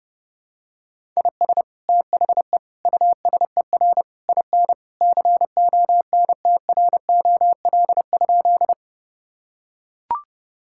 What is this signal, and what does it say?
Morse code, heard as a keyed tone, about 30 wpm, 700 Hz. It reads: IS THE USER IN CONTROL?
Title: Sentences from Top 1000 Words - Review Part 5 - 30wpm